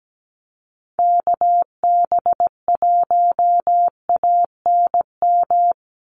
Morse code, heard as a keyed tone, about 17 wpm, 700 Hz. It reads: KB1ANM
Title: USA Call Signs 17wpm